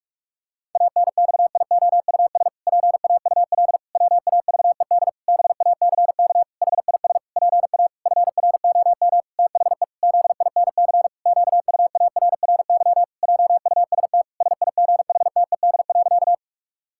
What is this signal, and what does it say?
Morse code, heard as a keyed tone, about 46 wpm, 700 Hz. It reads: ANXIOUS PAUL WAVED BACK HIS PA FROM THE ZINC QUARRY JUST SIGHTED.
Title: Pangram Challenge v2 - 45wpm